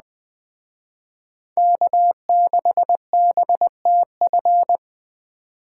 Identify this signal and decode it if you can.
Morse code, heard as a keyed tone, about 21 wpm, 700 Hz. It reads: K6BTF